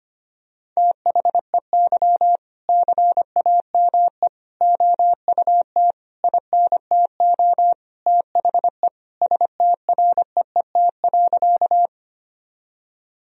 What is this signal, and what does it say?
Morse code, heard as a keyed tone, about 25 wpm, 700 Hz. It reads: THEY CAME OUT INTO THE STREET.
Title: Sentences from Top 1000 Words - Review Part 3 - 25wpm